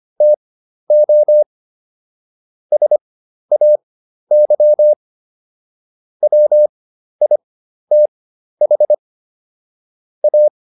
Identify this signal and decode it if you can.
Morse code, heard as a keyed tone, about 25 wpm, spaced as for 12 wpm, 600 Hz. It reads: TO SAY WITH A